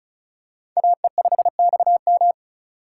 Morse code, heard as a keyed tone, about 34 wpm, 700 Hz. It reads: AE5XM
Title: USA Call Signs - 5 Characters - 3x Character Spacing - 35wpm